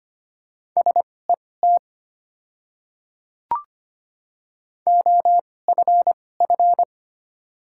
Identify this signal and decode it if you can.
Morse code, heard as a keyed tone, about 25 wpm, spaced as for 18 wpm, 690 Hz. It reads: SET OFF
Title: 2x Character Spacing - Top 200 Words - 25wpm